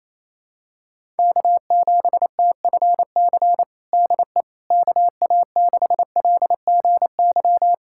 Morse code, heard as a keyed tone, about 28 wpm, 700 Hz. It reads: K7TFC DE KA6LGY
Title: USA Callsign DE Callsign 28wpm